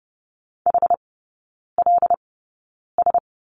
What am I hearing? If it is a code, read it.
Morse code, heard as a keyed tone, about 30 wpm, spaced as for 9 wpm, 690 Hz. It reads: HLS
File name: Lesson 16e - Letter c - Three Characters 30wpm